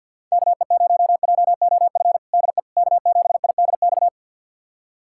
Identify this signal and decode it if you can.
Morse code, heard as a keyed tone, about 46 wpm, 700 Hz. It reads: KE0JOF DE K7IDX
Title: USA Callsign DE Callsign 50wpm